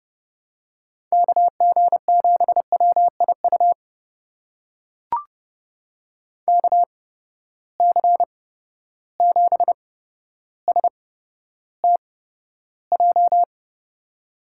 Morse code, heard as a keyed tone, about 32 wpm, spaced as for 13 wpm, 700 Hz. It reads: KG7WIU KC7STJ
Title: USA Call Signs - 6 Characters - 8x Character Spacing - 30wpm